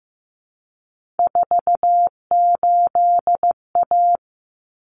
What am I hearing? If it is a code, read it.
Morse code, heard as a keyed tone, about 15 wpm, 700 Hz. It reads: 48A
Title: Common Cut Numbers - 3-Digit 15wpm